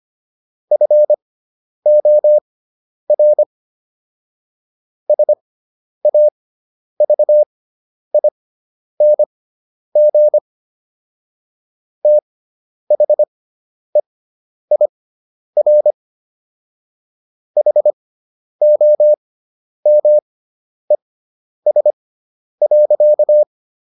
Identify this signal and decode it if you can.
Morse code, heard as a keyed tone, about 25 wpm, spaced as for 10 wpm, 600 Hz. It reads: FOR SAVING THEIR HOMES.